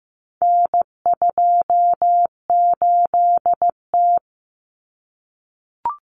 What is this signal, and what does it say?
Morse code, heard as a keyed tone, about 15 wpm, 700 Hz. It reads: N28T
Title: Common Cut Numbers - 4-Digit 15wpm